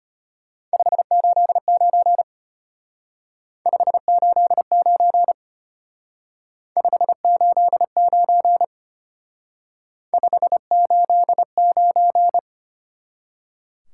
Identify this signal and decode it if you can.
Morse code, heard as a keyed tone, about 32 wpm, 700 Hz. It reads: 589 589 589 589